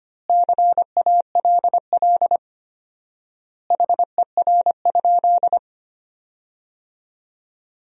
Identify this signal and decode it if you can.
Morse code, heard as a keyed tone, about 25 wpm, 700 Hz. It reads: CALL HER?
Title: 4x Word-Spacing - Sentences from Top 100 Words - 25wpm